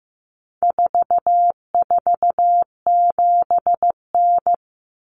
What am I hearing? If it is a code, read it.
Morse code, heard as a keyed tone, about 15 wpm, 700 Hz. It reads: 447N
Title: Common Cut Numbers - 4-Digit 15wpm